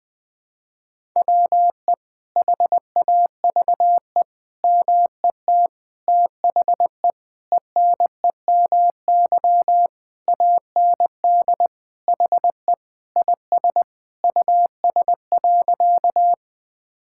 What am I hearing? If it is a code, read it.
Morse code, heard as a keyed tone, about 21 wpm, 700 Hz. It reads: WE HAVE MET THE ENEMY AND HE IS US.
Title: Sentences with 9-Word Phrases - 20wpm